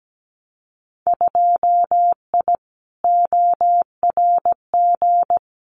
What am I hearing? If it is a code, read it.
Morse code, heard as a keyed tone, about 17 wpm, 700 Hz. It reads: 2I ORG
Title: Winter Field Day Exchanges Encore 17wpm